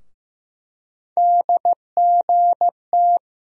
Morse code, heard as a keyed tone, about 15 wpm, 700 Hz. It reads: DGT
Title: USA Call Sign Postfixes - 3 Letters 15wpm